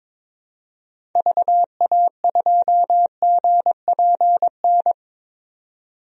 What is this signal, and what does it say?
Morse code, heard as a keyed tone, about 21 wpm, 700 Hz. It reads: VA2GPN
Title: DXCC-100 Call Signs - 6 Characters - 22wpm